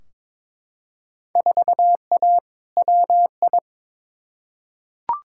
Morse code, heard as a keyed tone, about 22 wpm, 700 Hz. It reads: 4A WI
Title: Field Day Exchanges - 22wpm